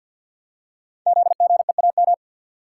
Morse code, heard as a keyed tone, about 52 wpm, 700 Hz. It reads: ZGEAM
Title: Code Groups - 5 Letters - 50wpm